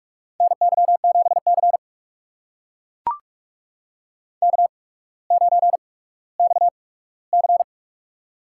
Morse code, heard as a keyed tone, about 44 wpm, spaced as for 19 wpm, 690 Hz. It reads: NY7C K9XC
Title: USA Call Signs - 4 Characters - 8x Character Spacing - 45wpm